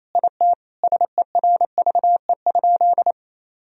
Morse code, heard as a keyed tone, about 28 wpm, 700 Hz. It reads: IT SERVE?
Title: Sentences from Top 1000 Words - Review Part 2 - 28wpm